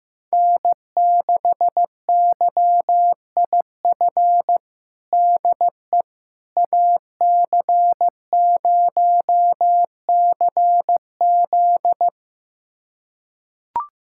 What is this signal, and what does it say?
Morse code, heard as a keyed tone, about 15 wpm, 700 Hz. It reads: N6YIF DE AC0CZ
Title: USA Callsign DE Callsign - 5 Characters - 15wpm